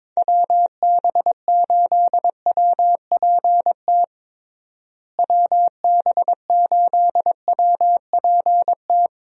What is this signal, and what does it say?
Morse code, heard as a keyed tone, about 21 wpm, 700 Hz. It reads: WB8WPT WB8WPT
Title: USA Call Signs - 6 Characters - Repeated 3x - 22wpm